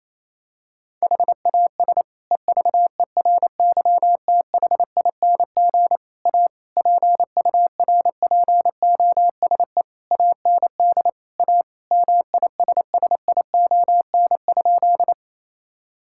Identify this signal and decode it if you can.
Morse code, heard as a keyed tone, about 28 wpm, 700 Hz. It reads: HAS EVERYTHING A PURPOSE AND A MISSION?
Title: Sentences from Top 1600 Words 28wpm